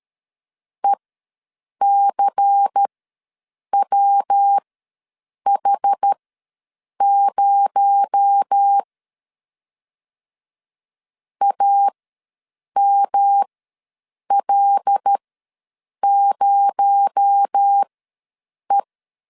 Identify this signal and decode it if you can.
Morse code, heard as a keyed tone, about 13 wpm, spaced as for 7 wpm, 800 Hz. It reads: ECWH0 AML0E
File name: LEZIONI / 7 GF9Z / 7 G ws 12wpm 3140